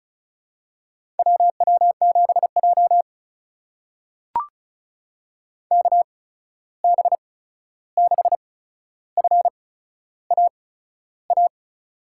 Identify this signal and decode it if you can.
Morse code, heard as a keyed tone, about 36 wpm, spaced as for 13 wpm, 700 Hz. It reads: WW7J KB6FAA
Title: USA Call Signs - 8x Character Spacing - 35wpm